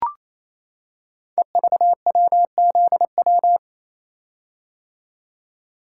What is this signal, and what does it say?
Morse code, heard as a keyed tone, about 27 wpm, 700 Hz. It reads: EVWZW
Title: Code Groups - 5 Letter:Numbers - 28wpm